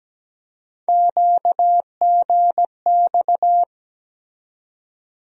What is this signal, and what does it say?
Morse code, heard as a keyed tone, about 17 wpm, 700 Hz. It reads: QGX
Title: USA Call Sign Postfixes - 3 Letters 17wpm